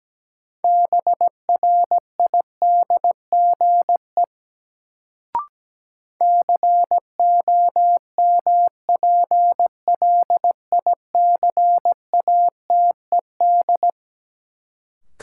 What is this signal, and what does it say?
Morse code, heard as a keyed tone, about 17 wpm, 700 Hz. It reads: BRIDGE COMPLICATED
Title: Sentences from Top 1200 Words - New Word Drill Alt - 17wpm